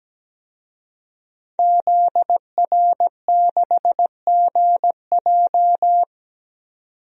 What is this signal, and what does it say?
Morse code, heard as a keyed tone, about 17 wpm, 700 Hz. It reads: ZR6GJ
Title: DXCC-100 Call Signs - 5 Characters - 17wpm